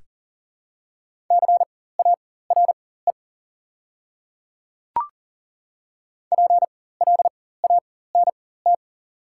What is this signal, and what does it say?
Morse code, heard as a keyed tone, about 40 wpm, spaced as for 19 wpm, 700 Hz. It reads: CARE PLANT
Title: 4x Character Spacing - Top 300 Words - 40wpm